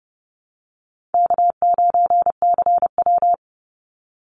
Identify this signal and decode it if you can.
Morse code, heard as a keyed tone, about 30 wpm, 700 Hz. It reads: K9CW
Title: Easy Call Signs - 30wpm